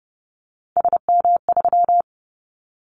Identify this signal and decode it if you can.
Morse code, heard as a keyed tone, about 30 wpm, 700 Hz. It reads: SM3
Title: Callsign Prefixes for Top 25 DXCC - 30wpm